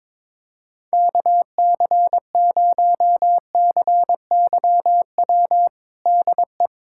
Morse code, heard as a keyed tone, about 22 wpm, 700 Hz. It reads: KC0CYW DE
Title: USA Callsign DE Callsign - 6 Characters - 22wpm